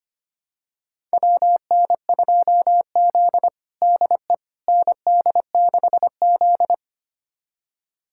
Morse code, heard as a keyed tone, about 25 wpm, 700 Hz. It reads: WN2Z DE ND6Z
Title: USA Callsign DE Callsign - 4 Characters - 25wpm